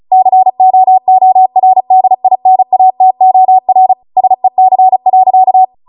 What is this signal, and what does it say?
Morse code, heard as a keyed tone, about 36 wpm, 750 Hz. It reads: COORDINATOR SEC.